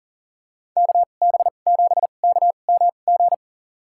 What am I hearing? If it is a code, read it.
Morse code, heard as a keyed tone, about 40 wpm, spaced as for 29 wpm, 700 Hz. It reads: KB7KMG
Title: USA Call Signs - 6 Characters - 2x Character Spacing - 40wpm